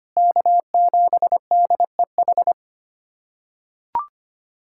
Morse code, heard as a keyed tone, about 25 wpm, 700 Hz. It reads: K7DEH
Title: USA Callsign DE Callsign - 5 Characters - 25wpm